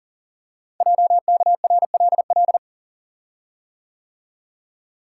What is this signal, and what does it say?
Morse code, heard as a keyed tone, about 40 wpm, 700 Hz. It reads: JKRLL
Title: Code Groups - 5 Letters - 40wpm